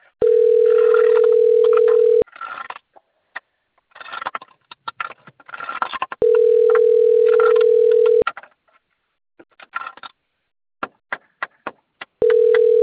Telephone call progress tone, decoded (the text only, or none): ringback tone